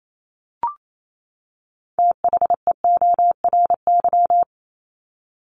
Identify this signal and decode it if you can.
Morse code, heard as a keyed tone, about 28 wpm, 700 Hz. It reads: THEORY
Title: Top 501-750 Words 28wpm